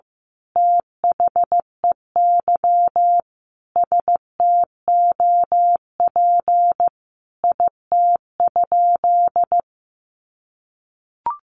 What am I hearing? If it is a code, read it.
Morse code, heard as a keyed tone, about 15 wpm, 700 Hz. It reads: THEY STOP IT?
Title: Sentences from Top 300 Words 15wpm